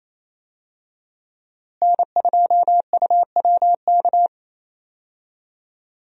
Morse code, heard as a keyed tone, about 27 wpm, 700 Hz. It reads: N2UWK
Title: USA Call Signs - 5 Characters - Repeated 3x - 28wpm